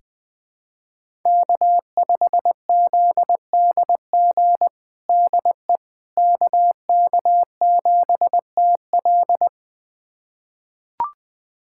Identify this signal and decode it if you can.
Morse code, heard as a keyed tone, about 20 wpm, 700 Hz. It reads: K5ZDG DE KK7TL
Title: USA Callsign DE Callsign - 5 Characters - 20wpm